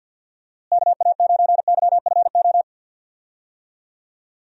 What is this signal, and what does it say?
Morse code, heard as a keyed tone, about 48 wpm, 700 Hz. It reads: KA9YFO
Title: USA Call Signs - 6 Characters - Repeated 2x - 50wpm